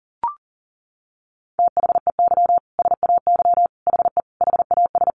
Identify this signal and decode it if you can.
Morse code, heard as a keyed tone, about 40 wpm, 700 Hz. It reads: THEY SAY HE HAS
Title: Sentences from Top 1100 Words 40wpm